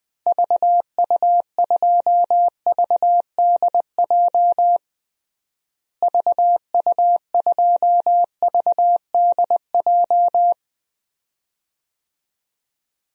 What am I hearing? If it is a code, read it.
Morse code, heard as a keyed tone, about 21 wpm, 700 Hz. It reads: VU2VDJ VU2VDJ
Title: DXCC-100 Call Signs - 6 Characters - Repeated 2x - 20wpm